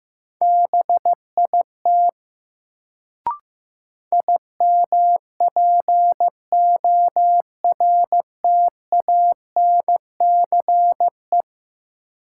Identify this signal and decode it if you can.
Morse code, heard as a keyed tone, about 15 wpm, 700 Hz. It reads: BIT IMPORTANCE
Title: Sets of 3 Words - New Word Drill - 15wpm